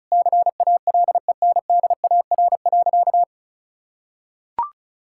Morse code, heard as a keyed tone, about 36 wpm, 700 Hz. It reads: CALENDAR.
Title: Sentences from Top 1600 Words 35wpm